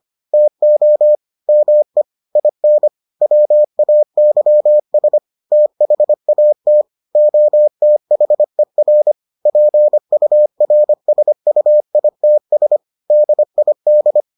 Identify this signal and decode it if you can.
Morse code, heard as a keyed tone, about 25 wpm, 600 Hz. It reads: TO ME IN WAYS THAT OTHER PURSUITS DID